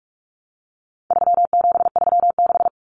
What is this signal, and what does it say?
Morse code, heard as a keyed tone, about 44 wpm, 700 Hz. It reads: 3736
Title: Numbers - 4-Digit 45wpm